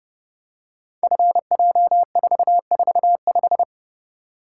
Morse code, heard as a keyed tone, about 30 wpm, 700 Hz. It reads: FJ445
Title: Code Groups - 5 Letter:Numbers - 30wpm